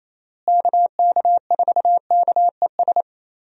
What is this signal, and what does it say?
Morse code, heard as a keyed tone, about 28 wpm, 700 Hz. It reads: KK4KES
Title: USA Call Signs - 6 Characters - Repeated 2x - 28wpm